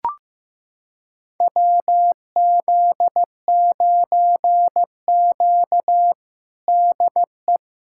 Morse code, heard as a keyed tone, about 15 wpm, 700 Hz. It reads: WZ9Q DE
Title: USA Callsign DE Callsign 15wpm